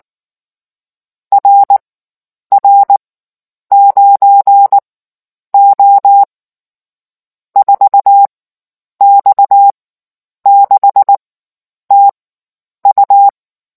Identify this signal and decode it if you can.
Morse code, heard as a keyed tone, about 19 wpm, spaced as for 9 wpm, 800 Hz. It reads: RR9O 4X6TU